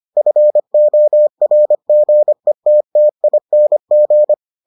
Morse code, heard as a keyed tone, about 25 wpm, 600 Hz. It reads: FORGETTING